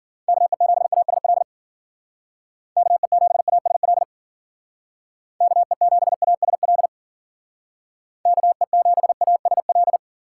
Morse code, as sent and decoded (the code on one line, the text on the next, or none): -.- . --... .- ... .-.. / -.- . --... .- ... .-.. / -.- . --... .- ... .-.. / -.- . --... .- ... .-..
KE7ASL KE7ASL KE7ASL KE7ASL